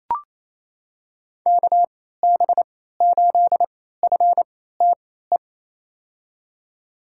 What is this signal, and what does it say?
Morse code, heard as a keyed tone, about 28 wpm, spaced as for 16 wpm, 700 Hz. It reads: KB8FTE
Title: USA Call Signs - 3x Character Spacing - 28wpm